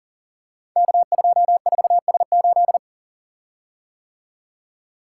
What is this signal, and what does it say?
Morse code, heard as a keyed tone, about 42 wpm, 700 Hz. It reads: K24S8
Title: Code Groups - 5 Letter:Numbers - 40wpm